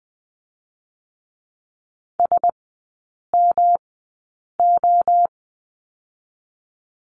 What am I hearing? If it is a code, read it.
Morse code, heard as a keyed tone, about 20 wpm, spaced as for 8 wpm, 700 Hz. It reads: SMO
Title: 107 Lesson 18e - Letter m - Three Characters 20wpm